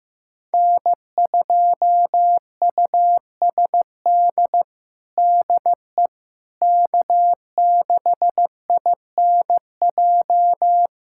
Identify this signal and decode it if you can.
Morse code, heard as a keyed tone, about 15 wpm, 700 Hz. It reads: N2USD DE K6INJ
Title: USA Callsign DE Callsign - 5 Characters - 15wpm